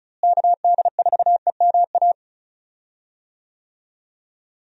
Morse code, heard as a keyed tone, about 36 wpm, 700 Hz. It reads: KD4EMA
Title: USA Call Signs - 6 Characters - Repeated 2x - 35wpm